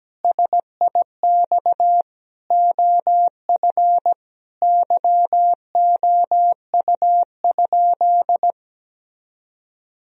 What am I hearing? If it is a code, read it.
Morse code, heard as a keyed tone, about 17 wpm, 700 Hz. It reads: SIX OF YOU?
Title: Sentences from Top 1000 Words - Review Part 2 - 17wpm